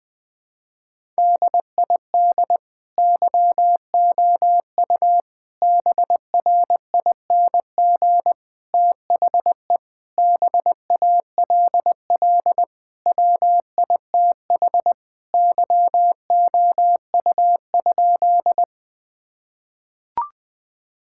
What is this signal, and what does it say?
Morse code, heard as a keyed tone, about 21 wpm, 700 Hz. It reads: DID YOU BRING THE BALL WITH YOU?
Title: Sentences from Top 1000 Words - Review Part 6 - 20wpm